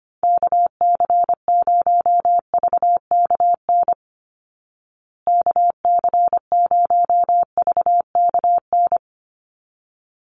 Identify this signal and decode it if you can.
Morse code, heard as a keyed tone, about 25 wpm, 700 Hz. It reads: KC0VKN KC0VKN
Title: Call Signs Repeated 3x - 25wpm